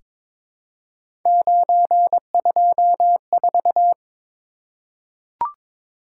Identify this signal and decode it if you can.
Morse code, heard as a keyed tone, about 23 wpm, 700 Hz. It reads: 924
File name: practice / 22wpm / Three Letter-Number Combinations - 22wpm